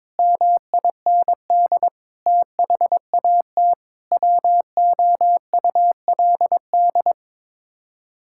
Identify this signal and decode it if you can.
Morse code, heard as a keyed tone, about 22 wpm, 700 Hz. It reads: MIND THAT WOULD
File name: Sets of 9 Words - 22wpm